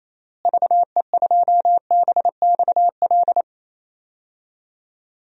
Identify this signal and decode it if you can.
Morse code, heard as a keyed tone, about 28 wpm, 700 Hz. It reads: VE2BXL
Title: DXCC-100 Call Signs - 6 Characters - Repeated 2x - 28wpm